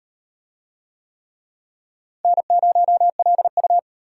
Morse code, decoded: N0LU

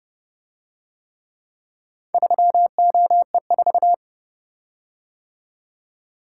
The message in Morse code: 3OE4

